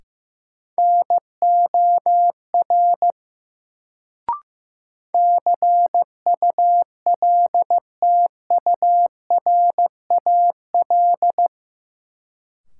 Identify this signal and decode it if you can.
Morse code, heard as a keyed tone, about 15 wpm, 700 Hz. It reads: NOR CULTURAL
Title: Sentences from Top 1500 Words - New Word Drill Alt - 15wpm